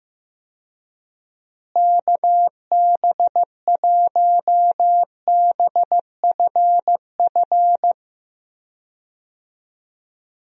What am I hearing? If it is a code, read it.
Morse code, heard as a keyed tone, about 15 wpm, 700 Hz. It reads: KB1BFF
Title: USA Call Signs - 6 Characters - Repeated 3x - 15wpm